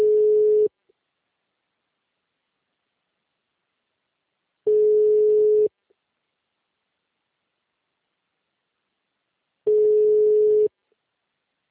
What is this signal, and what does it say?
Ringback tone (Europe, 425 Hz).